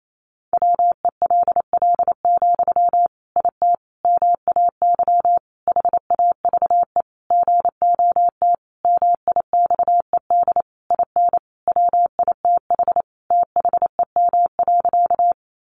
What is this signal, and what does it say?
Morse code, heard as a keyed tone, about 28 wpm, 700 Hz. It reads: WELL, IT MAY HAVE GOT MIXED IN WITH THEM.
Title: Sentences from Top 1500 Words 28wpm